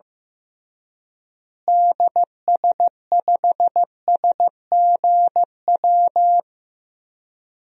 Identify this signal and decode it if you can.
Morse code, heard as a keyed tone, about 15 wpm, 700 Hz. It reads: DS5SGW
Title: DXCC-100 Call Signs - 6 Characters - Repeated 3x - 15wpm